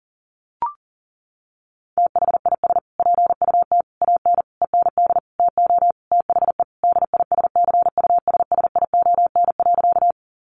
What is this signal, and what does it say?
Morse code, heard as a keyed tone, about 40 wpm, 700 Hz. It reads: THIS PUT AN END TO THE DISCUSSION.
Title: Sentences from Top 1100 Words 40wpm